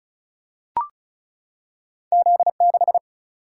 Morse code, heard as a keyed tone, about 35 wpm, 700 Hz. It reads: Z6
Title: Single-Letter Single-Number 35wpm